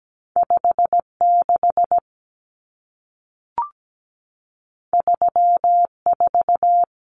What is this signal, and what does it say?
Morse code, heard as a keyed tone, about 17 wpm, 700 Hz. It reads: 56 34
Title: Numbers - 2-Digit 17wpm